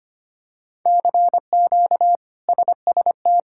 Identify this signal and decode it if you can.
Morse code, heard as a keyed tone, about 25 wpm, 700 Hz. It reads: CQ SST
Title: K1USN SST QSO Practice - 25wpm